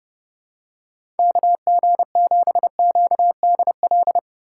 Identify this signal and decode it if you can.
Morse code, heard as a keyed tone, about 30 wpm, 700 Hz. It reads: KG7QDL
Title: USA Call Signs - 6 Characters - 4x Character Spacing - 30wpm